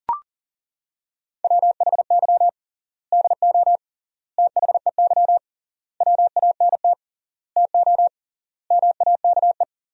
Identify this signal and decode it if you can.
Morse code, heard as a keyed tone, about 40 wpm, 700 Hz. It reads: WHY DO THEY WANT TO MAKE
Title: 3x Word-Spacing - Sentences from Top 400 Words - 40wpm